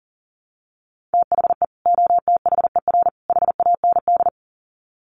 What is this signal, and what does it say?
Morse code, heard as a keyed tone, about 40 wpm, 700 Hz. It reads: THE OTHER HAND
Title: Sets of 3 Words 40wpm